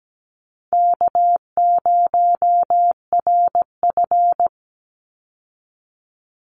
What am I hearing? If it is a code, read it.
Morse code, heard as a keyed tone, about 17 wpm, 700 Hz. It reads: K0RF
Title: Easy Call Signs Repeated 3x - 17wpm